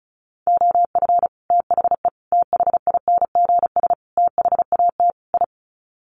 Morse code, heard as a keyed tone, about 34 wpm, 700 Hz. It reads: OF THE THINGS THAT I
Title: Sets of 5 Words Encore 35wpm